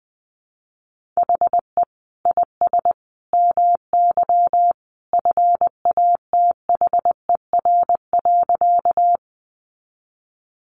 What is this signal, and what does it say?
Morse code, heard as a keyed tone, about 20 wpm, 700 Hz. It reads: HE IS MY FATHER.
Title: Sentences from Top 300 Words 20wpm